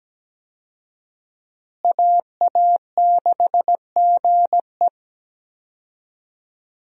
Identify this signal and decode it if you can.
Morse code, heard as a keyed tone, about 17 wpm, 700 Hz. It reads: AA6GE
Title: USA Call Signs - 5 Characters - Repeated 3x - 17wpm